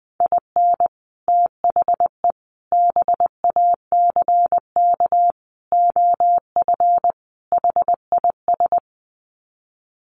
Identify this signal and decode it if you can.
Morse code, heard as a keyed tone, about 20 wpm, 700 Hz. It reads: IN THE BACK OF HIS
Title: Sets of 5 Words Encore2 20wpm